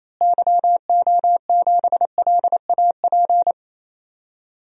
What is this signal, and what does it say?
Morse code, heard as a keyed tone, about 28 wpm, 700 Hz. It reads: YO7LAP
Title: DXCC-100 Call Signs - 6 Characters - Repeated 3x - 28wpm